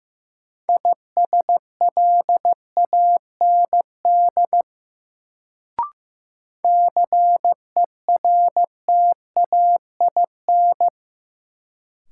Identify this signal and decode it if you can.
Morse code, heard as a keyed tone, about 15 wpm, 700 Hz. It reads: ISLAND CERTAIN
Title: Sentences from Top 500 Words - New Word Drill - 15wpm